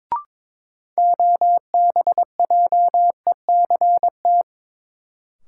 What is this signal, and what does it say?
Morse code, heard as a keyed tone, about 22 wpm, 700 Hz. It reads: OBJECT
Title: Sentences from Top 500 Words - New Word Drill - 22wpm